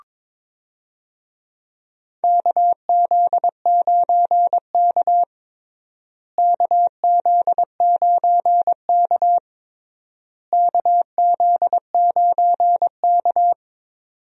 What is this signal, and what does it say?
Morse code, heard as a keyed tone, about 22 wpm, 700 Hz. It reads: KZ9K KZ9K KZ9K
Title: USA Call Signs - 4 Characters - Repeated 3x - 22wpm